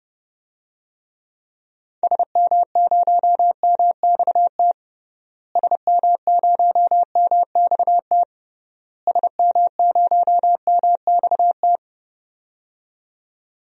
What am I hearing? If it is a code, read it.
Morse code, heard as a keyed tone, about 30 wpm, 700 Hz. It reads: SM0MXT SM0MXT SM0MXT
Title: DXCC-100 Call Signs - 6 Characters - Repeated 3x - 30wpm